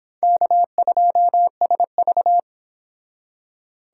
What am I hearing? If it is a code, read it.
Morse code, heard as a keyed tone, about 31 wpm, 700 Hz. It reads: K2SV